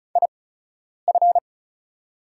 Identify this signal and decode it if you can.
Morse code, heard as a keyed tone, about 36 wpm, spaced as for 10 wpm, 690 Hz. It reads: IF
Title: 8x Character Spacing - Top 100 Words - 35wpm